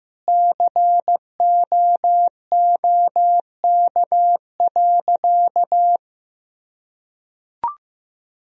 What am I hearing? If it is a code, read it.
Morse code, heard as a keyed tone, about 15 wpm, 700 Hz. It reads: COOK.
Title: Sentences from Top 1000 Words - Review Part 4 - 15wpm